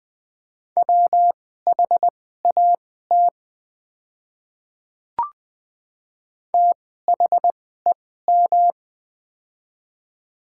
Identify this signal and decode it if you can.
Morse code, heard as a keyed tone, about 21 wpm, spaced as for 15 wpm, 700 Hz. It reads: WHAT THEM
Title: 2x Character Spacing - Top 100 Words - 20wpm